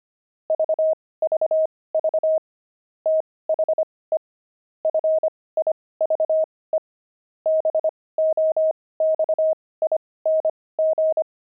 Morse code, heard as a keyed tone, about 25 wpm, spaced as for 18 wpm, 630 Hz. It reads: VVV THE FIVE BOXING
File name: pangram-fs-18-cs-25-wizards